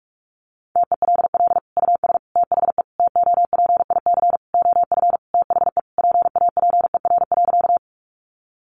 Morse code, heard as a keyed tone, about 44 wpm, 700 Hz. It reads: TELL US THE TOPIC OF THE PAPER.
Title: Sentences from Top 1200 Words 45wpm